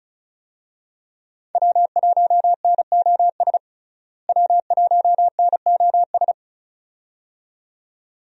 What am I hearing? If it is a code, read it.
Morse code, heard as a keyed tone, about 35 wpm, 700 Hz. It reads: W1NOS W1NOS
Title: USA Call Signs - 5 Characters - Repeated 2x - 35wpm